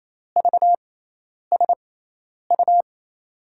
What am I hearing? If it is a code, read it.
Morse code, heard as a keyed tone, about 28 wpm, spaced as for 10 wpm, 700 Hz. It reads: VSU
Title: USA Call Signs - 6x Character Spacing - 28wpm